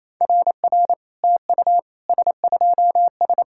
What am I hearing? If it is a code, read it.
Morse code, heard as a keyed tone, about 27 wpm, 700 Hz. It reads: RR TU S2S
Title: SOTA QSO Practice 28wpm